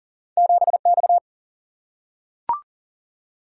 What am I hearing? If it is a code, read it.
Morse code, heard as a keyed tone, about 39 wpm, 700 Hz. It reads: 7 X
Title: Single-Letter Single-Number 40wpm